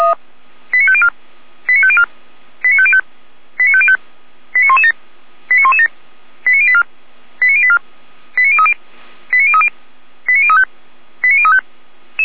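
Five-tone selective call (ZVEI), heard as quick runs of five tones, tones of about 70 ms. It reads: 89504 89504 89505 89505 89117 89117 89084 89084 89030 89030 89035 89035
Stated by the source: ZVEI five-tone call received on an SDR.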